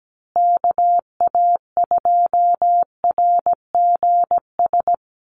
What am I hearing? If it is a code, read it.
Morse code, heard as a keyed tone, about 17 wpm, 700 Hz. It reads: KA2RGS
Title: USA Call Signs Repeated 3x 17wpm